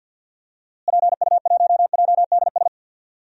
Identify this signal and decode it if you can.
Morse code, heard as a keyed tone, about 50 wpm, 700 Hz. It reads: PU1JDS
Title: DXCC-100 Call Signs - 6 Characters - Repeated 2x - 50wpm